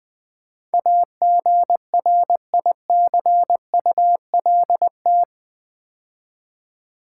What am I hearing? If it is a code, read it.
Morse code, heard as a keyed tone, about 21 wpm, 700 Hz. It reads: AGRICULT
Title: Build A Word - 4 Letters At A Time - 11-Letter Words - 20wpm